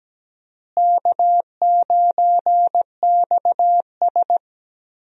Morse code, heard as a keyed tone, about 17 wpm, 700 Hz. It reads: K9XS
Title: USA Call Signs - 4 Characters - Repeated 3x - 17wpm